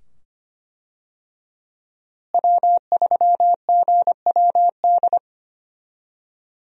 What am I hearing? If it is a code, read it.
Morse code, heard as a keyed tone, about 25 wpm, 700 Hz. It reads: W3GWD
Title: USA Call Signs - 5 Characters - Repeated 2x - 25wpm